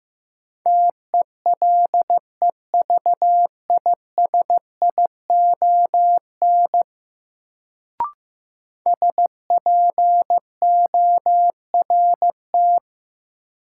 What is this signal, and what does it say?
Morse code, heard as a keyed tone, about 15 wpm, 700 Hz. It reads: TELEVISION SPORT
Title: Sentences from Top 1000 Words - Review Part 4 - New Word Drill - 15wpm